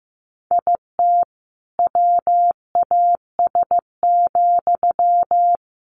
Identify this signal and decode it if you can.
Morse code, heard as a keyed tone, about 15 wpm, 700 Hz. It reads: IT WAS,
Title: Sentences from Top 1500 Words 15wpm